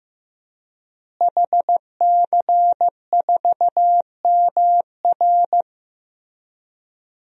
Morse code, heard as a keyed tone, about 15 wpm, 700 Hz. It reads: HC4MR
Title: DXCC-100 Call Signs - 5 Characters - Repeated 2x - 15wpm